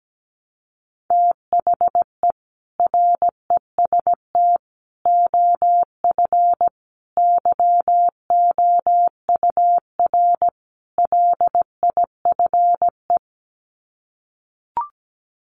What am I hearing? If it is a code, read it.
Morse code, heard as a keyed tone, about 17 wpm, 700 Hz. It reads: THE REST OF YOUR LIFE
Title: Sets of 5 Words 17wpm